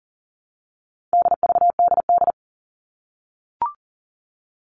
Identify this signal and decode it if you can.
Morse code, heard as a keyed tone, about 40 wpm, 700 Hz. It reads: DVDD
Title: All Cut Numbers - 4-Digit 40wpm